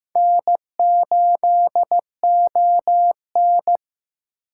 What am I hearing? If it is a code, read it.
Morse code, heard as a keyed tone, about 15 wpm, 700 Hz. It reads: N8ON
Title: USA Call Signs - 4 Characters - 3x Character Spacing - 15wpm